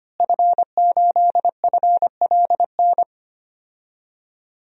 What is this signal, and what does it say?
Morse code, heard as a keyed tone, about 25 wpm, 700 Hz. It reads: F8FLN
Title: DXCC-100 Call Signs - 5 Characters - Repeated 2x - 25wpm